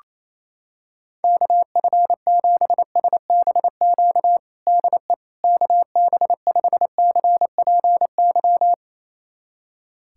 Morse code, heard as a keyed tone, about 28 wpm, 700 Hz. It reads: KF7SBQ DE KB5CPY
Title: USA Callsign DE Callsign 28wpm